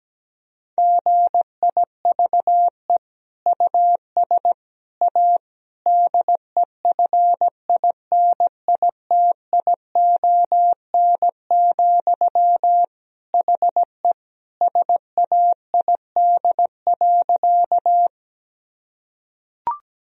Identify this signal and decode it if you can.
Morse code, heard as a keyed tone, about 17 wpm, 700 Hz. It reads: GIVE US A DEFINITION, HE SAID.
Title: Sentences from Top 1000 Words - Review Part 4 - 17wpm